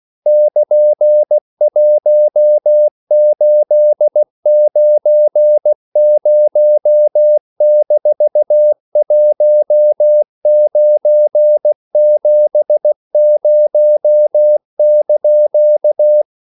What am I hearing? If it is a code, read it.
Morse code, heard as a keyed tone, about 16 wpm, 600 Hz. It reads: <KN>1890-1970)